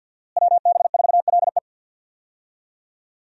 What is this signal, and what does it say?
Morse code, heard as a keyed tone, about 47 wpm, 700 Hz. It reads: WB4LE
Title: USA Call Signs - 5 Characters - Repeated 2x - 50wpm